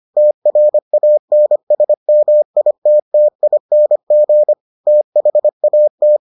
Morse code, heard as a keyed tone, about 25 wpm, 600 Hz. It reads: TRANSMITTING THAT